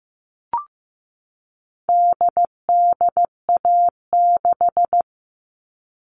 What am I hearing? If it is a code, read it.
Morse code, heard as a keyed tone, about 15 wpm, 700 Hz. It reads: DDA6
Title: All Cut Numbers - 4-Digit 15wpm